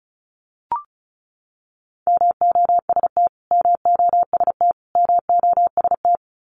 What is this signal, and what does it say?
Morse code, heard as a keyed tone, about 34 wpm, 700 Hz. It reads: MOST MOST MOST
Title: Top 500 Words Repeated 35wpm